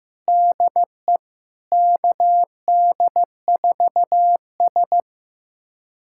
Morse code, heard as a keyed tone, about 15 wpm, 700 Hz. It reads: DE KD4S
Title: USA Callsign DE Callsign - 4 Characters - 15wpm